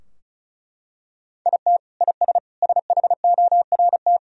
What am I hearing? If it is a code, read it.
Morse code, heard as a keyed tone, about 36 wpm, 700 Hz. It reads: IT IS SHORT
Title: Sentences from Top 1000 Words - Review Part 2 - 35wpm